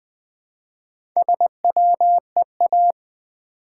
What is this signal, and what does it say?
Morse code, heard as a keyed tone, about 20 wpm, 700 Hz. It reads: SWEA
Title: Build A Word - 1 Letter At A Time - 5-Letter Words - 20wpm